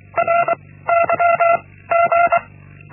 Morse code, heard as a keyed tone, about 24 wpm, 680 Hz. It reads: R Y G
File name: Linksmakalnis-1